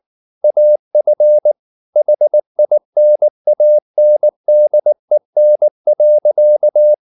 Morse code, heard as a keyed tone, about 19 wpm, 600 Hz. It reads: AF HINANDEN.
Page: Ferie 19 wpm